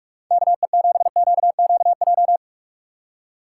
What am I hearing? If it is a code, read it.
Morse code, heard as a keyed tone, about 44 wpm, 700 Hz. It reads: KE7QQJ